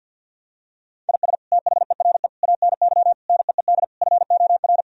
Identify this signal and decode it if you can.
Morse code, heard as a keyed tone, about 50 wpm, 700 Hz. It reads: IS THERE ANY NEED FOR
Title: Sentences from Top 1000 Words - Review Part 4 - 50wpm